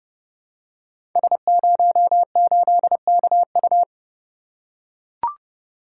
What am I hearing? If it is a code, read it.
Morse code, heard as a keyed tone, about 29 wpm, 700 Hz. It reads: S08KU
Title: Code Groups - 5 Letter:Numbers - 30wpm